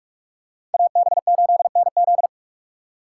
Morse code, heard as a keyed tone, about 46 wpm, 700 Hz. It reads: AB8NZ
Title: USA Call Signs - 5 Characters - 4x Character Spacing - 45wpm